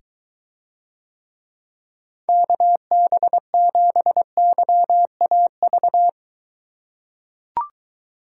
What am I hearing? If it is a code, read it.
Morse code, heard as a keyed tone, about 23 wpm, 700 Hz. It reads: KB7YAV